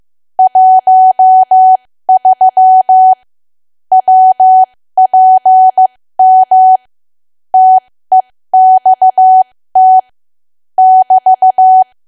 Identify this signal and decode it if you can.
Morse code, heard as a keyed tone, about 15 wpm, spaced as for 13 wpm, 750 Hz. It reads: 13 WPM TEXT <BT>
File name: cwa / W1AW-Level2 / 13WPM